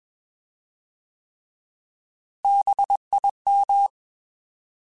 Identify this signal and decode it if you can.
Morse code, heard as a keyed tone, about 21 wpm, 800 Hz. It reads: BIM
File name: Bs @20